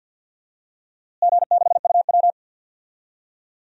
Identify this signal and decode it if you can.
Morse code, heard as a keyed tone, about 47 wpm, 700 Hz. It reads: G6UW
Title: DXCC-100 Call Signs - 4 Characters - Repeated 2x - 50wpm